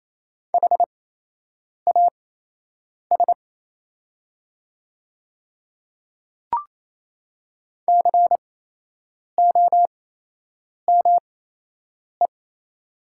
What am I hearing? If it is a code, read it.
Morse code, heard as a keyed tone, about 28 wpm, spaced as for 8 wpm, 700 Hz. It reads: HAS COME
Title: 8x Character Spacing - Top 100 Words - 28wpm